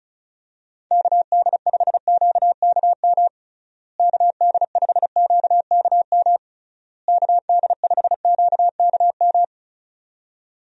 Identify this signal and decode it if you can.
Morse code, heard as a keyed tone, about 34 wpm, 700 Hz. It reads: KD5QKM KD5QKM KD5QKM
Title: USA Call Signs - 6 Characters - Repeated 3x - 35wpm